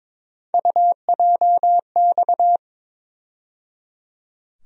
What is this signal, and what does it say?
Morse code, heard as a keyed tone, about 22 wpm, 700 Hz. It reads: UJX
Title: USA Call Sign Postfixes - 3 Letters 22wpm